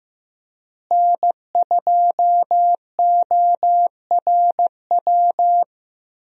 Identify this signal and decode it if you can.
Morse code, heard as a keyed tone, about 15 wpm, 700 Hz. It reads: N2ORW
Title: USA Call Signs - 5 Characters - Repeated 3x - 15wpm